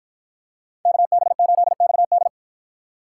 Morse code, heard as a keyed tone, about 53 wpm, 700 Hz. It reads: KB8XD